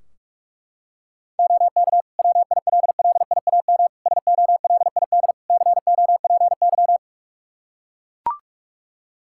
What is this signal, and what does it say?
Morse code, heard as a keyed tone, about 41 wpm, 700 Hz. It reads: OK WILLIAM SOLID COPY
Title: Practice QSO - Exchange 3 - More Info - 45wpm